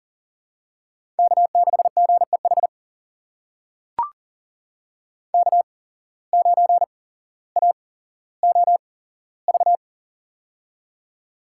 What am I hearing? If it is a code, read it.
Morse code, heard as a keyed tone, about 38 wpm, spaced as for 17 wpm, 700 Hz. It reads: K6GEH K9AOV
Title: USA Call Signs - 5 Characters - 8x Character Spacing - 40wpm